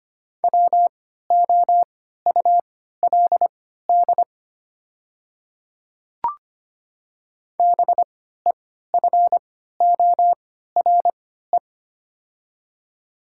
Morse code, heard as a keyed tone, about 25 wpm, spaced as for 14 wpm, 700 Hz. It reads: WOULD BEFORE
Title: 3x Character Spacing - Top 200 Words - 25wpm